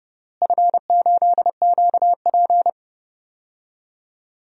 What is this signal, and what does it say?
Morse code, heard as a keyed tone, about 30 wpm, 700 Hz. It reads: F8QP
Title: DXCC-100 Call Signs - 4 Characters - Repeated 3x - 30wpm